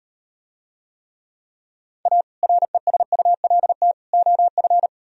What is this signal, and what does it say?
Morse code, heard as a keyed tone, about 38 wpm, 700 Hz. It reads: A RESULT OF